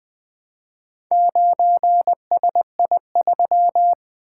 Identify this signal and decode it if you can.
Morse code, heard as a keyed tone, about 20 wpm, 700 Hz. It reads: 9SI3
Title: Four Letter-Number Combinations - 20wpm